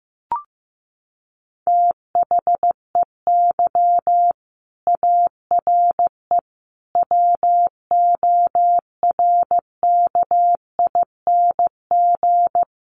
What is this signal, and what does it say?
Morse code, heard as a keyed tone, about 15 wpm, 700 Hz. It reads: THEY ARE WORKING
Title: Sentences from Top 700 Words 15wpm